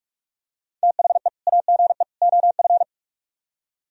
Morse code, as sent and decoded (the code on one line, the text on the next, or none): - .... / .- --. / --- ..-.
THE AGE OF